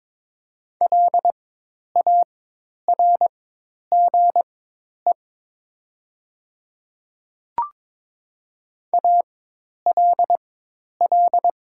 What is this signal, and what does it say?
Morse code, heard as a keyed tone, about 22 wpm, spaced as for 10 wpm, 700 Hz. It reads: LARGE ALL
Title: 4x Character Spacing - Top 500 Words - 22wpm